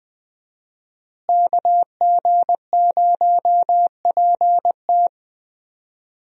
Morse code, heard as a keyed tone, about 20 wpm, 700 Hz. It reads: KG0PT
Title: USA Call Signs - 5 Characters - Repeated 3x - 20wpm